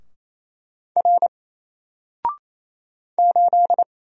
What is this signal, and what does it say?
Morse code, heard as a keyed tone, about 27 wpm, 700 Hz. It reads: R 8
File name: Single-Letter-Number-28wpm